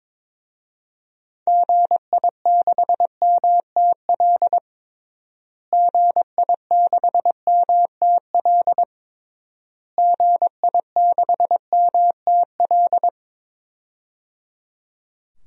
Morse code, heard as a keyed tone, about 22 wpm, 700 Hz. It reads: GI6MTL GI6MTL GI6MTL
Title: DXCC-100 Call Signs - 6 Characters - Repeated 3x - 22wpm